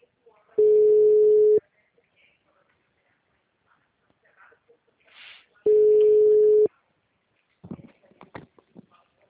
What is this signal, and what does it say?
Ringback tone (Europe, 425 Hz), with speech.